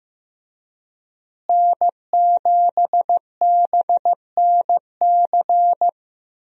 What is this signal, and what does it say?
Morse code, heard as a keyed tone, about 15 wpm, 700 Hz. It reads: N7BNC
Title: USA Call Signs - 5 Characters - Repeated 2x - 15wpm